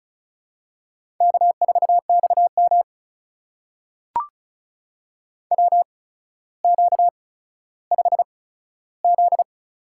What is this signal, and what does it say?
Morse code, heard as a keyed tone, about 36 wpm, spaced as for 15 wpm, 700 Hz. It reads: K4XM WQ5Z
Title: USA Call Signs - 4 Characters - 8x Character Spacing - 35wpm